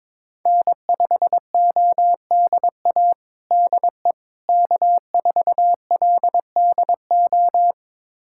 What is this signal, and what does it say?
Morse code, heard as a keyed tone, about 22 wpm, 700 Hz. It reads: N5ODA DE K4LDO
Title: USA Callsign DE Callsign - 5 Characters - 22wpm